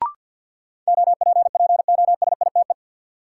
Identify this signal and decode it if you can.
Morse code, heard as a keyed tone, about 53 wpm, 700 Hz. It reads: OPPOSITE
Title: Sentences from Top 1300 Words - New Word Drill Alt - 50wpm